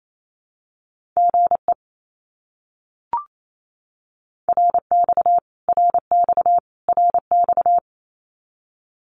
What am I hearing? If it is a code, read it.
Morse code, heard as a keyed tone, about 28 wpm, 700 Hz. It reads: GE RX RX RX
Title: Abbreviations Repeated 3x - 28wpm